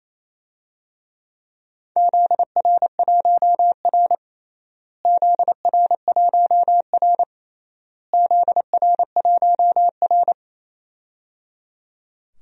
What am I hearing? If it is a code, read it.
Morse code, heard as a keyed tone, about 28 wpm, 700 Hz. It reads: ZR1R ZR1R ZR1R
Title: DXCC-100 Call Signs - 4 Characters - Repeated 3x - 28wpm